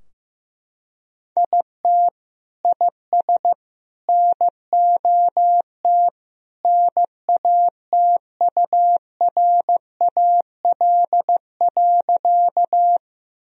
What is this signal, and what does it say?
Morse code, heard as a keyed tone, about 15 wpm, 700 Hz. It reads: IT IS NOT NATURAL.
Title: Sentences from Top 1000 Words - Review Part 4 - 15wpm